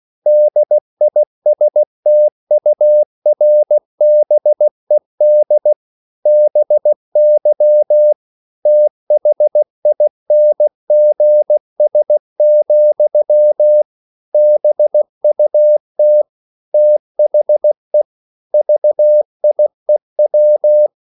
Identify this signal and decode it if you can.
Morse code, heard as a keyed tone, about 16 wpm, 600 Hz. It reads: DISTURBED BY THINGS, BUT THE VIEW